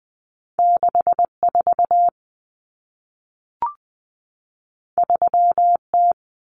Morse code, heard as a keyed tone, about 20 wpm, 700 Hz. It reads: 64 3T
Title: Common Cut Numbers - 2-Digit 20wpm